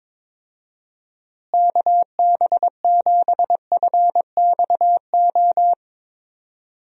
Morse code, heard as a keyed tone, about 22 wpm, 700 Hz. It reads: KB7FXO
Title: USA Call Signs - 6 Characters - Repeated 2x - 22wpm